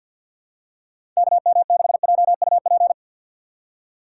Morse code, heard as a keyed tone, about 48 wpm, 700 Hz. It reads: KM6JUP
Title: USA Call Signs - 6 Characters - 4x Character Spacing - 50wpm